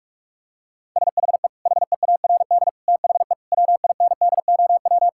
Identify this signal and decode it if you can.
Morse code, heard as a keyed tone, about 46 wpm, 700 Hz. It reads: SHE HEARD THE WINDOW
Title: Sentences from Top 1600 Words 45wpm